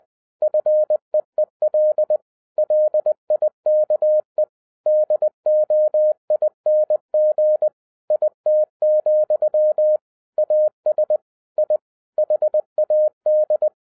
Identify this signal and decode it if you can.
Morse code, heard as a keyed tone, about 20 wpm, 610 Hz. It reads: FEEL LIKE DOING IT, AS I HAD
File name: 2012-01-mp3 / n16-THROUGH EIGHT MONTHS-20wpm